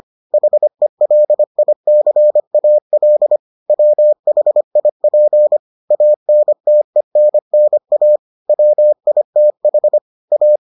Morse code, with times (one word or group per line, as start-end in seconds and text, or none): 0.3-3.4 HELICAL
3.7-5.6 WHIP
5.9-8.2 ANTENNA
8.5-10.0 WITH
10.3-10.6 A